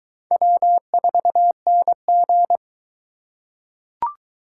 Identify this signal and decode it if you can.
Morse code, heard as a keyed tone, about 20 wpm, 700 Hz. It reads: W4NG